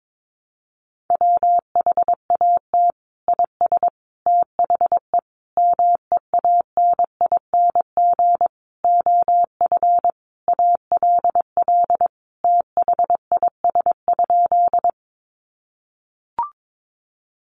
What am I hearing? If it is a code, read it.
Morse code, heard as a keyed tone, about 22 wpm, 700 Hz. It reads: WHAT IS THE MEANING OF ALL THIS?
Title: Sentences from Top 900 Words 22wpm